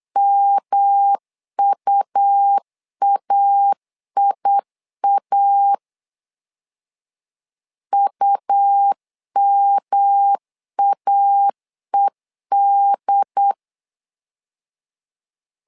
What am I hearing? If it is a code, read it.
Morse code, heard as a keyed tone, about 8 wpm, 800 Hz. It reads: MUAIA UMAED
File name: LEZIONI / 3 MWRO / 3 M 8wpm 2300